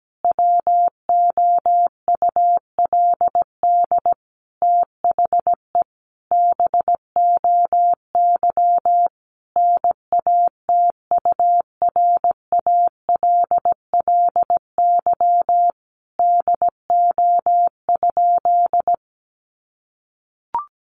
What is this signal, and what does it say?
Morse code, heard as a keyed tone, about 17 wpm, 700 Hz. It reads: WOULD THE BOY NATURALLY DO?
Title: Sentences from Top 1300 Words 17wpm